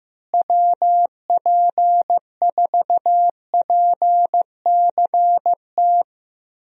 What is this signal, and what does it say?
Morse code, heard as a keyed tone, about 16 wpm, 700 Hz. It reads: WP4PCT
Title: USA Call Signs - 6 Characters - Repeated 3x - 15wpm